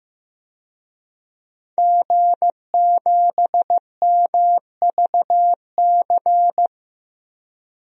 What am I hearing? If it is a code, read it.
Morse code, heard as a keyed tone, about 15 wpm, 700 Hz. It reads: G7MVC
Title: DXCC-100 Call Signs - 5 Characters - Repeated 3x - 15wpm